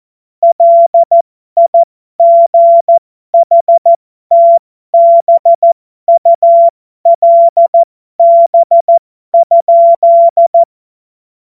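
Morse code, heard as a keyed tone, about 14 wpm, spaced as for 12 wpm, 680 Hz. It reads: LIGHTBULB?
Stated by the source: Morse code joke practice.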